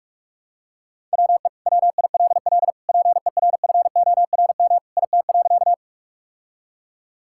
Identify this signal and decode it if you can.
Morse code, heard as a keyed tone, about 46 wpm, 700 Hz. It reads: WE WILL PERFORM IT.
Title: Sentences from Top 1000 Words - Review Part 5 - 45wpm